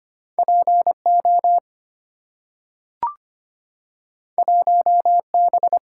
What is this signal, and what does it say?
Morse code, heard as a keyed tone, about 25 wpm, 700 Hz. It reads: P O 1 B
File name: Two Letter-Number Combinations - 25wpm